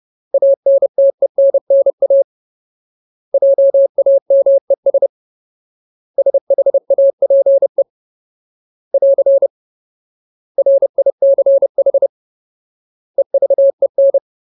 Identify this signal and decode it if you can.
Morse code, heard as a keyed tone, about 30 wpm, 550 Hz. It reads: ANTENNA JAMES SHAPE <AR> RICH EVEN
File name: Ultimate Word List / 30WPM Ultimate